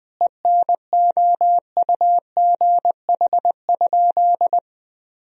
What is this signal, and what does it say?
Morse code, heard as a keyed tone, about 20 wpm, 700 Hz. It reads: ENOUGH?
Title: Sentences from Top 1000 Words - Review Part 2 - 20wpm